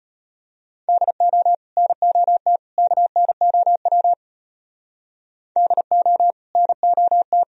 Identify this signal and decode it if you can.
Morse code, heard as a keyed tone, about 32 wpm, 700 Hz. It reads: DO NOT KNOW DO NOT